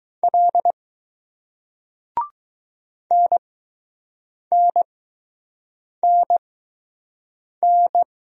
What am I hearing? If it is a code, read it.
Morse code, heard as a keyed tone, about 20 wpm, spaced as for 6 wpm, 700 Hz. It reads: L NNNN